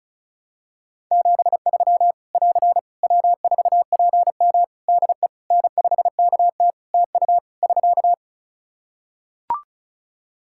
Morse code, heard as a keyed tone, about 36 wpm, 700 Hz. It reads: 73 <AR> W4PM DE N5KT TU <SK>
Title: K1USN SST QSO Practice 35wpm